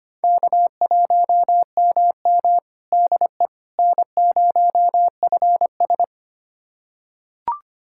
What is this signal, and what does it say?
Morse code, heard as a keyed tone, about 25 wpm, 700 Hz. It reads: K1MM DE N0FS
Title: USA Callsign DE Callsign - 4 Characters - 25wpm